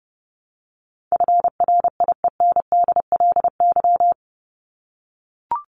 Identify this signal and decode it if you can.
Morse code, heard as a keyed tone, about 30 wpm, 700 Hz. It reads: FRIENDLY
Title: Top 1751-2000 Words 30wpm